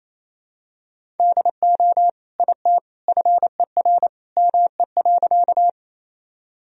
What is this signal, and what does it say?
Morse code, heard as a keyed tone, about 28 wpm, 700 Hz. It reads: DO IT FER ME.
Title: Sentences from Top 200 Words and Abbreviations 28wpm